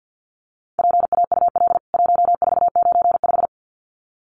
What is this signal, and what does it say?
Morse code, heard as a keyed tone, about 50 wpm, 700 Hz. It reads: PAUL 1495
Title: CWT Exchange 50wpm